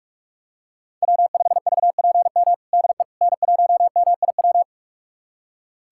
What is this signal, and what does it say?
Morse code, heard as a keyed tone, about 41 wpm, 700 Hz. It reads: W5VPM DE N1MIW
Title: USA Callsign DE Callsign - 5 Characters - 45wpm